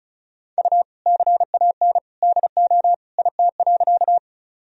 Morse code, heard as a keyed tone, about 34 wpm, 700 Hz. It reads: U CAN DO IT.
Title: Sentences from Top 100 Words and Abbreviations 35wpm